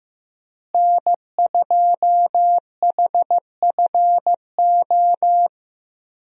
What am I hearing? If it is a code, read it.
Morse code, heard as a keyed tone, about 15 wpm, 700 Hz. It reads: N2HFO
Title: USA Call Signs - 5 Characters - Repeated 2x - 15wpm